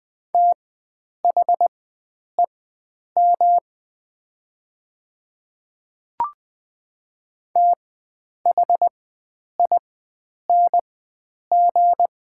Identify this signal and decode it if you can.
Morse code, heard as a keyed tone, about 20 wpm, spaced as for 9 wpm, 700 Hz. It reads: THEM THING
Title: 4x Character Spacing - Top 100 Words - 20wpm